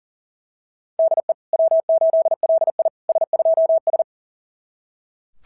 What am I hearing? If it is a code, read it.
Morse code, heard as a keyed tone, about 34 wpm, 660 Hz. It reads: DE W8LI S2S
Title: SOTA QSO Practice 40wpm